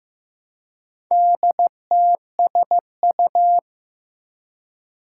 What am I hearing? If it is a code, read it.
Morse code, heard as a keyed tone, about 15 wpm, 700 Hz. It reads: DTSU
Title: Four Challenging Letter-Number Combinations 15wpm